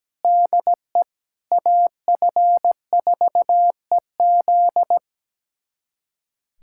Morse code, heard as a keyed tone, about 17 wpm, 700 Hz. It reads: DE AF4EZ
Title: USA Callsign DE Callsign - 5 Characters - 17wpm